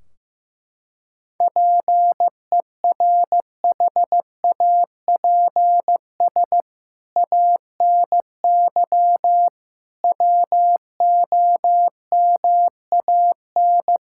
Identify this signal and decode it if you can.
Morse code, heard as a keyed tone, about 15 wpm, 700 Hz. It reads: PERHAPS ANY WOMAN
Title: Sentences from Top 1600 Words 15wpm